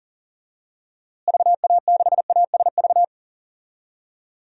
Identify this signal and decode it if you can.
Morse code, heard as a keyed tone, about 40 wpm, 700 Hz. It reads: VA6ASV
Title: DXCC-100 Call Signs - 6 Characters - Repeated 2x - 40wpm